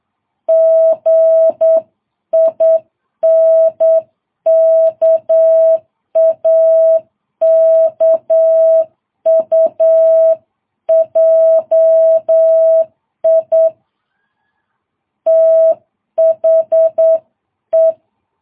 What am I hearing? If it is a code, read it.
Morse code, heard as a keyed tone, about 8 wpm, 660 Hz. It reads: GINKAKUJI THE
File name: jh8cbh